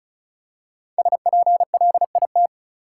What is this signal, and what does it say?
Morse code, heard as a keyed tone, about 34 wpm, 700 Hz. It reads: SPLIT
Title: Five Letter Words - 35wpm